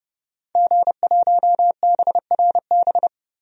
Morse code, heard as a keyed tone, about 30 wpm, 700 Hz. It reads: G1BRB
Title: DXCC-100 Call Signs - 5 Characters - 30wpm